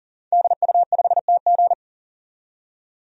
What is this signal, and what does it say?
Morse code, heard as a keyed tone, about 40 wpm, 700 Hz. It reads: DU5TG
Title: DXCC-100 Call Signs - 5 Characters - 40wpm